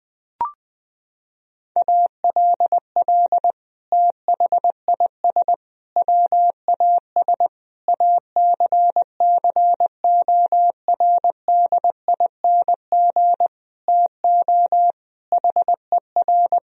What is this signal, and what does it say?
Morse code, heard as a keyed tone, about 20 wpm, 700 Hz. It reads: ALL THIS WAS ACCORDING TO HER
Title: Sentences from Top 1000 Words - Review Part 4 - 20wpm